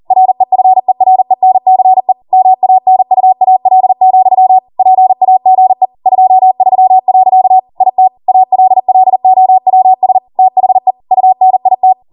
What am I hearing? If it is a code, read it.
Morse code, heard as a keyed tone, about 40 wpm, 750 Hz. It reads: REFERENCE MANUAL, PAGE 23. IT ALLOWS THE UNIT